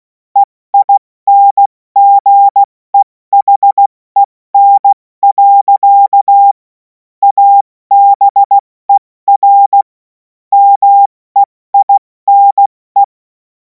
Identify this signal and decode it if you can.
Morse code, heard as a keyed tone, about 16 wpm, 800 Hz. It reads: EINGEHEN. ABER MEINE
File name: Morsekurs / Witze / Witz 26. Mai 2026 70 F80